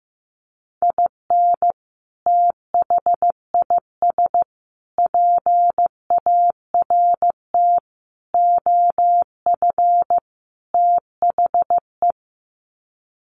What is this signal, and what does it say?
Morse code, heard as a keyed tone, about 15 wpm, 700 Hz. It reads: IN THIS PART OF THE
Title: Sets of 5 Words 15wpm